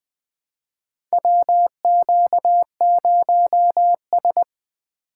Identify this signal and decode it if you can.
Morse code, heard as a keyed tone, about 20 wpm, 700 Hz. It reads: WQ0S
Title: USA Call Signs - 4 Characters - 3x Character Spacing - 20wpm